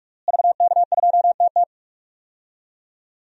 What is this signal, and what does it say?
Morse code, heard as a keyed tone, about 46 wpm, 700 Hz. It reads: VK2TT
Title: DXCC-100 Call Signs - 5 Characters - Repeated 3x - 45wpm